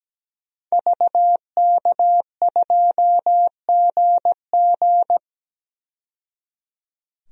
Morse code, heard as a keyed tone, about 17 wpm, 700 Hz. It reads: VK2GG
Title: DXCC-100 Call Signs - 5 Characters - 17wpm